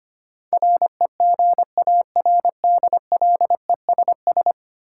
Morse code, heard as a keyed tone, about 25 wpm, 700 Hz. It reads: REGARDLESS
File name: Ten Letter Words - 25wpm